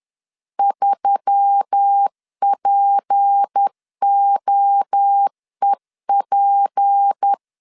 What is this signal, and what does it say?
Morse code, heard as a keyed tone, about 10 wpm, 800 Hz. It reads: 3POEP
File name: LEZIONI / 6 PKX0 / 6 P 10wpm 2432